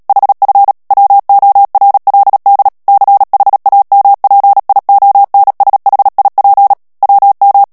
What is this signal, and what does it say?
Morse code, heard as a keyed tone, about 37 wpm, 800 Hz. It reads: HF WORLD CHAMPIONSHIP WM
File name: morse mp3 / ALLEMAND texte clair 2 vitesse 35 mots minute